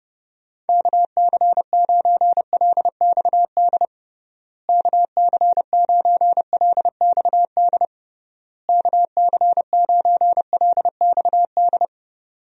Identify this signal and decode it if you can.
Morse code, heard as a keyed tone, about 31 wpm, 700 Hz. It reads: KC9LXD KC9LXD KC9LXD
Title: USA Call Signs - 6 Characters - Repeated 3x - 30wpm